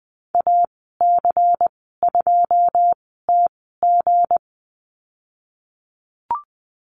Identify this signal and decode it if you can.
Morse code, heard as a keyed tone, about 20 wpm, spaced as for 15 wpm, 700 Hz. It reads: AC2TG
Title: 173 Lesson 28g - Number 9 - Call Signs 20wpm